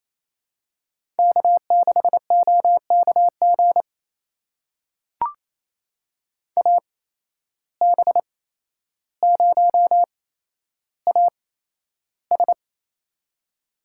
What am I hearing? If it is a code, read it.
Morse code, heard as a keyed tone, about 28 wpm, spaced as for 12 wpm, 700 Hz. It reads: K6OKG AB0AS
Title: USA Call Signs - 5 Characters - 8x Character Spacing - 28wpm